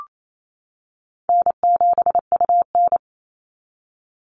Morse code, heard as a keyed tone, about 28 wpm, 700 Hz. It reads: N7UN
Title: NA SOTA Chaser Callsigns 28wpm